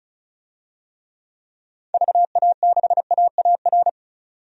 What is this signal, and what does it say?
Morse code, heard as a keyed tone, about 36 wpm, 700 Hz. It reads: VA6AAR
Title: DXCC-100 Call Signs - 6 Characters - 35wpm